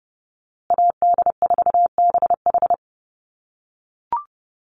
Morse code, heard as a keyed tone, about 30 wpm, 700 Hz. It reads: AD4BH
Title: USA Call Signs Repeated 2x 30wpm